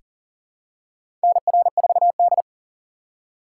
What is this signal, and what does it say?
Morse code, heard as a keyed tone, about 40 wpm, 700 Hz. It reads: NR4D
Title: USA Call Signs - 4 Characters - 3x Character Spacing - 40wpm